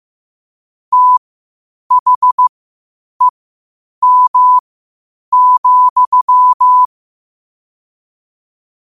Morse code, heard as a keyed tone, about 15 wpm, spaced as for 8 wpm, 1000 Hz. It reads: THEM,
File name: qotdfiles / 8wpm